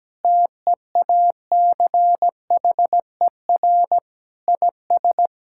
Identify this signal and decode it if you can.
Morse code, heard as a keyed tone, about 17 wpm, 700 Hz. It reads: TEACHER IS
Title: Sentences from Top 1600 Words 17wpm